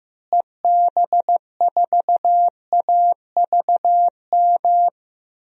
Morse code, heard as a keyed tone, about 15 wpm, 700 Hz. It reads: EB4AVM
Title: DXCC-100 Call Signs - 6 Characters - Repeated 2x - 15wpm